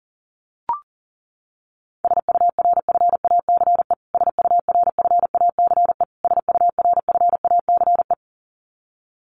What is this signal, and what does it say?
Morse code, heard as a keyed tone, about 40 wpm, 700 Hz. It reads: SURFACE SURFACE SURFACE
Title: Top 500 Words Repeated 40wpm